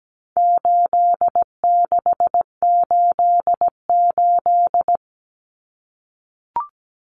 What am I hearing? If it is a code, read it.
Morse code, heard as a keyed tone, about 17 wpm, 700 Hz. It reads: 8688
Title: Numbers - 4-Digit 17wpm